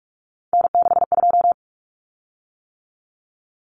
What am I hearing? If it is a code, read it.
Morse code, heard as a keyed tone, about 44 wpm, 700 Hz. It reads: N62
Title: Common Cut Numbers - 3-Digit 45wpm